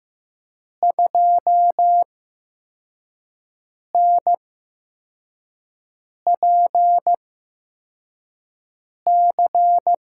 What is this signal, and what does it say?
Morse code, heard as a keyed tone, about 15 wpm, spaced as for 4 wpm, 700 Hz. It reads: 2NPC
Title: USA Call Signs - 6 Characters - 8x Character Spacing - 15wpm